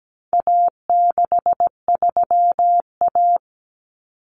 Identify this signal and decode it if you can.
Morse code, heard as a keyed tone, about 17 wpm, 700 Hz. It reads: A63A
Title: Common Cut Numbers - 4-Digit 17wpm